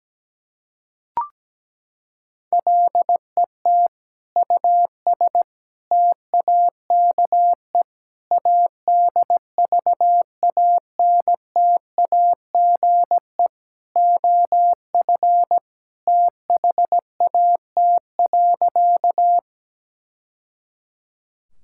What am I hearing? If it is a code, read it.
Morse code, heard as a keyed tone, about 17 wpm, 700 Hz. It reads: LET US TAKE ADVANTAGE OF THAT.
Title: Sentences from Top 1000 Words - Review Part 6 - 17wpm